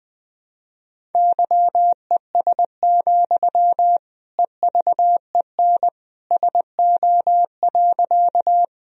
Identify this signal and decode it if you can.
Morse code, heard as a keyed tone, about 20 wpm, 700 Hz. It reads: YES, EVEN SO.
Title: Sentences from Top 1000 Words - Review Part 1 - 20wpm